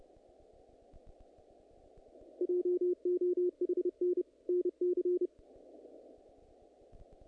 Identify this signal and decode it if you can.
Morse code, heard as a keyed tone, about 30 wpm, 360 Hz. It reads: JOHN NC
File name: contests / naqp-cw-k5tr-op-k5pi- -1900